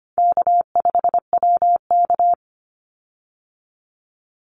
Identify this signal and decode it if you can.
Morse code, heard as a keyed tone, about 25 wpm, 700 Hz. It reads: K5WK
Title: Easy Call Signs Repeated 3x - 25wpm